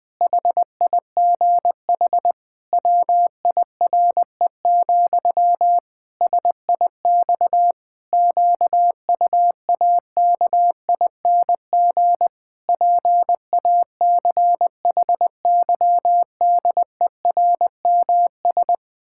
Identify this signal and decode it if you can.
Morse code, heard as a keyed tone, about 20 wpm, 700 Hz. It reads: HIGH WIRE, SIX QUAKING PACHYDERMS
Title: Pangram Challenge v2 - 20wpm